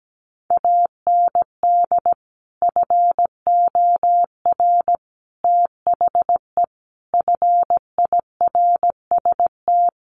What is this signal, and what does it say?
Morse code, heard as a keyed tone, about 17 wpm, 700 Hz. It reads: AND FOR THE FIRST
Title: Sets of 5 Words 17wpm